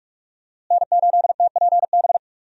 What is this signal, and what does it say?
Morse code, heard as a keyed tone, about 43 wpm, 700 Hz. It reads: N8TPB
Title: USA Call Signs - 5 Characters - Repeated 3x - 45wpm